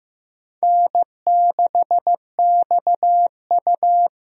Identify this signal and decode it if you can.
Morse code, heard as a keyed tone, about 15 wpm, 700 Hz. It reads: N6XU
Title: USA Call Signs - 4 Characters - Repeated 3x - 15wpm